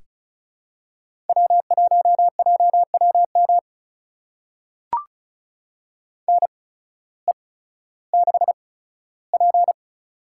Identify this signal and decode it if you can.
Morse code, heard as a keyed tone, about 36 wpm, spaced as for 16 wpm, 700 Hz. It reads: W1JWM NE6P
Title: USA Call Signs - 8x Character Spacing - 35wpm